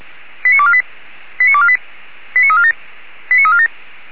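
Five-tone selective call (ZVEI), heard as quick runs of five tones, tones of about 70 ms. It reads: 89248 89248 89347 89347
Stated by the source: ZVEI five-tone selective call received on SDR.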